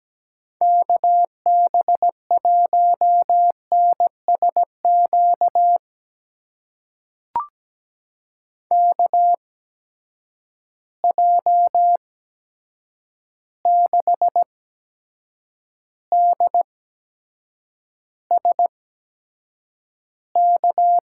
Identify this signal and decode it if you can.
Morse code, heard as a keyed tone, about 17 wpm, spaced as for 7 wpm, 700 Hz. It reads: KB1NSQ KJ6DSK
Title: USA Call Signs - 6 Characters - 8x Character Spacing - 17wpm